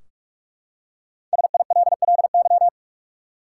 Morse code, heard as a keyed tone, about 46 wpm, 700 Hz. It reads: SILLY
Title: Five Letter Words - 45wpm